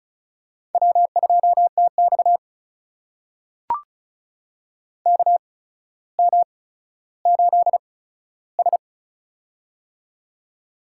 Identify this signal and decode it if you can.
Morse code, heard as a keyed tone, about 36 wpm, spaced as for 15 wpm, 700 Hz. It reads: W2TX KM8S
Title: USA Call Signs - 4 Characters - 8x Character Spacing - 35wpm